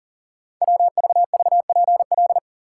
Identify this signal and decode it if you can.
Morse code, heard as a keyed tone, about 38 wpm, 700 Hz. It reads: WVVPL
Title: Code Groups - 5 Letter:Numbers - 40wpm